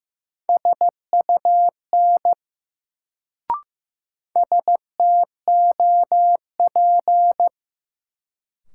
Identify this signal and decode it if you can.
Morse code, heard as a keyed tone, about 15 wpm, 700 Hz. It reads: SUN STOP
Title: Sentences from Top 300 Words - New Word Drill - 15wpm